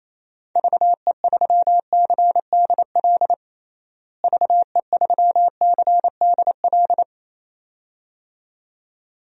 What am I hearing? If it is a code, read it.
Morse code, heard as a keyed tone, about 28 wpm, 700 Hz. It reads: VE3CDL VE3CDL
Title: DXCC-100 Call Signs - 6 Characters - Repeated 2x - 28wpm